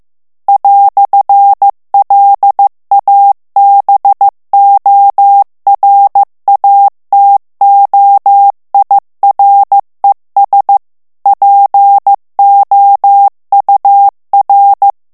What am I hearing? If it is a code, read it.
Morse code, heard as a keyed tone, about 15 wpm, 800 Hz. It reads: "LABORATOIRES POUR
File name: morse mp3 / FRANCAIS TEXTE CLAIR 1 VITESSE 14 MOTS MINUTES